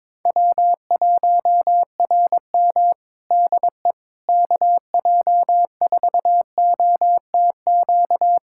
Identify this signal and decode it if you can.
Morse code, heard as a keyed tone, about 22 wpm, 700 Hz. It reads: W1RM DE KJ4OTQ
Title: Practice QSO - Exchange 4 - Ending - 22wpm